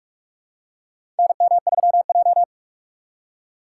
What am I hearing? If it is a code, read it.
Morse code, heard as a keyed tone, about 46 wpm, 700 Hz. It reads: NM3J
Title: USA Call Signs - 4 Characters - 3x Character Spacing - 45wpm